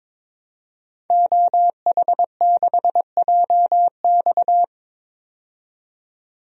Morse code, heard as a keyed tone, about 22 wpm, 700 Hz. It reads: OH6JX
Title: DXCC-100 Call Signs - 5 Characters - 22wpm